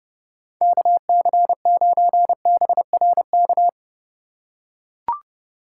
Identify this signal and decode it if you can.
Morse code, heard as a keyed tone, about 30 wpm, 700 Hz. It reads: KC9BRK
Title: USA Call Signs - 6 Characters - 3x Character Spacing - 30wpm